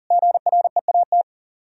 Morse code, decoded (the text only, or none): GREAT